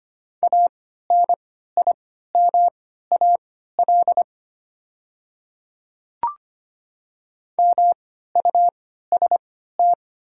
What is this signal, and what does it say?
Morse code, heard as a keyed tone, about 25 wpm, spaced as for 14 wpm, 690 Hz. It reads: ANIMAL MUST
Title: 3x Character Spacing - Top 200 Words - 25wpm